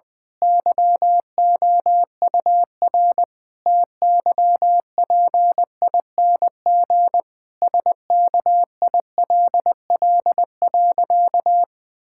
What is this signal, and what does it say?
Morse code, heard as a keyed tone, about 21 wpm, 700 Hz. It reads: YOUR TYPING SKILL.
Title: Pangram Challenge v2 - 20wpm